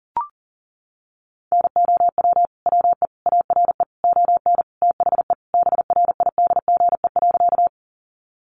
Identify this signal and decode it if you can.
Morse code, heard as a keyed tone, about 40 wpm, 700 Hz. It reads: NOW WE ARE ON THE BRIDGE.
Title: Sentences from Top 1200 Words 40wpm